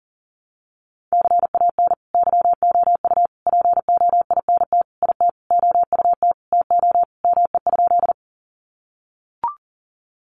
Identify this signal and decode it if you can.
Morse code, heard as a keyed tone, about 40 wpm, 700 Hz. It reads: CAN YOU POINT IT OUT TO ME?
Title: Sentences from Top 200 Words 40wpm